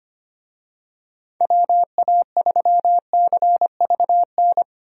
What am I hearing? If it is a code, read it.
Morse code, heard as a keyed tone, about 25 wpm, 700 Hz. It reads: WA3CVN
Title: USA Call Signs - 6 Characters - Repeated 3x - 25wpm